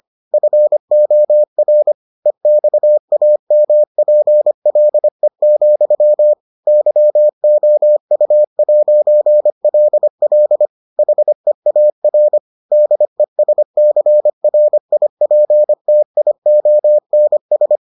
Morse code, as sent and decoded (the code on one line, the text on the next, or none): ..-. --- .-. / . -..- .- -- .--. .-.. --..-- / -.-- --- ..- .----. .-.. .-.. / .... .- .-. / -.. ... -.-. .-. .. .--. - .. --- -. ...
FOR EXAMPLE, YOU'LL HEAR DESCRIPTIONS